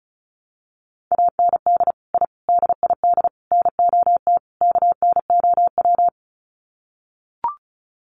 Morse code, heard as a keyed tone, about 34 wpm, 700 Hz. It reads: AND I DID NOT KNOW
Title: Sets of 5 Words Encore 35wpm